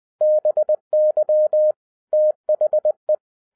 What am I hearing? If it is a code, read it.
Morse code, heard as a keyed tone, about 20 wpm, 610 Hz. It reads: BY THE